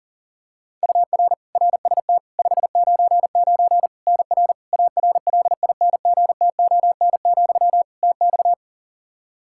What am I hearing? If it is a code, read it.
Morse code, heard as a keyed tone, about 37 wpm, 700 Hz. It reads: UR RST 599 NR ARLINGTON, TX
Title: Practice QSO - Exchange 1 - Beginning - 40wpm